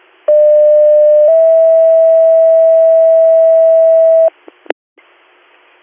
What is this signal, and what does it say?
Two-tone page, heard as a one-second tone, then a three-second tone, about 600 then 650 Hz. It is heard from 0.3 to 4.3 s.